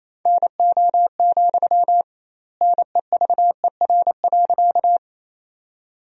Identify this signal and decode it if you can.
Morse code, heard as a keyed tone, about 28 wpm, 700 Hz. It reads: NO, NEVER.
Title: 2x Word-Spacing - Sentences from Top 300 Words - 28wpm